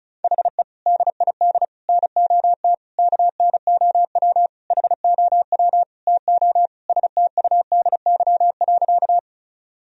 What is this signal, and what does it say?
Morse code, heard as a keyed tone, about 36 wpm, 700 Hz. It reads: HE DID NOT KNOW HOW TO STUDY.
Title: Sentences from Top 1000 Words - Review Part 1 - 35wpm